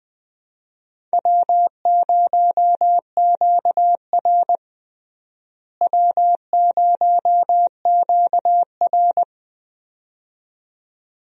Morse code, heard as a keyed tone, about 19 wpm, 700 Hz. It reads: W0QR W0QR
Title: USA Call Signs - 4 Characters - Repeated 2x - 20wpm